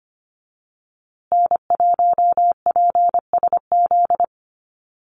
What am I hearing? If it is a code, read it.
Morse code, heard as a keyed tone, about 25 wpm, 700 Hz. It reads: N1PSZ
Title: USA Call Signs Repeated 3x 25wpm